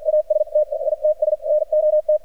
Morse code, heard as a keyed tone, about 53 wpm, spaced as for 44 wpm, 610 Hz, with radio noise. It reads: AST ITS NOT
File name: archived / CRSnet